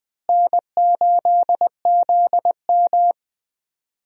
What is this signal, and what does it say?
Morse code, heard as a keyed tone, about 20 wpm, 700 Hz. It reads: N8ZM
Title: USA Call Signs - 4 Characters - Repeated 3x - 20wpm